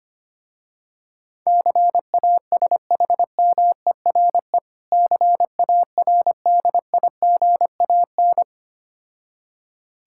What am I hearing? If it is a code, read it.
Morse code, heard as a keyed tone, about 25 wpm, 700 Hz. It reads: CASHMERE CARDIGAN
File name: practice / 25wpm / Difficult Words with Hints - 25wpm